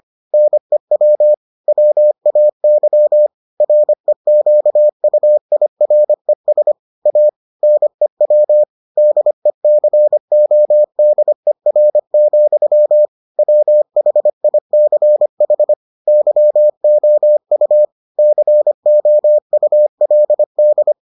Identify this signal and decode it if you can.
Morse code, heard as a keyed tone, about 25 wpm, 600 Hz. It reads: NEW WAY REQUIRES A NEW DECODER, WHICH YOU COULD